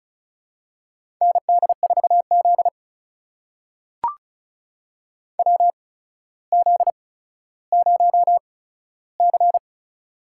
Morse code, heard as a keyed tone, about 36 wpm, spaced as for 15 wpm, 700 Hz. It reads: ND4Z WZ0C
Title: USA Call Signs - 4 Characters - 8x Character Spacing - 35wpm